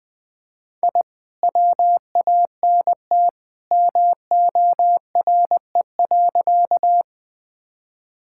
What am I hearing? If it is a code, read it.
Morse code, heard as a keyed tone, about 20 wpm, 700 Hz. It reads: I WANT MORE.
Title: Sentences with 2-Word Phrases - 20wpm